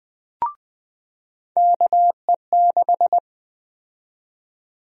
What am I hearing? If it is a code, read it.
Morse code, heard as a keyed tone, about 20 wpm, 700 Hz. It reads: KE6
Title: USA Call Sign Prefixes - 20wpm